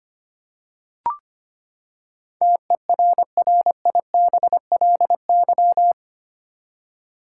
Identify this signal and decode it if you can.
Morse code, heard as a keyed tone, about 25 wpm, 700 Hz. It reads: TERRIBLY
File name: Eight Letter Words - 25wpm